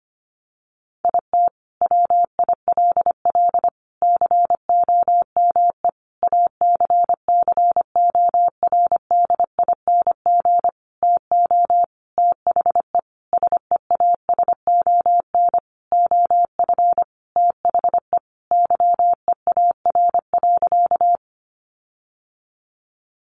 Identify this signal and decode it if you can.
Morse code, heard as a keyed tone, about 25 wpm, 700 Hz. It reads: IT WILL COME ACCORDING TO THE SEASON OF THE YEAR.
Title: Sentences from Top 1400 Words 25wpm